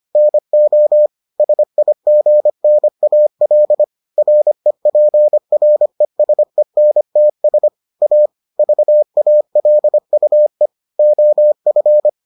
Morse code, heard as a keyed tone, about 25 wpm, 600 Hz. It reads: NO SIGNAL REPRESENTS A VALUE OF